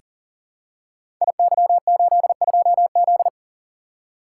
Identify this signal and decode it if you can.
Morse code, heard as a keyed tone, about 38 wpm, 700 Hz. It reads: IY82Z
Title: Code Groups - 5 Letter:Numbers - 40wpm